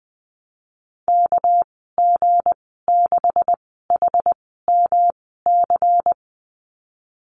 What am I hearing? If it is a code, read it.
Morse code, heard as a keyed tone, about 20 wpm, spaced as for 15 wpm, 700 Hz. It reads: KG6HMC
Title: 207 Lesson 33g - Letter j - Call Signs 20wpm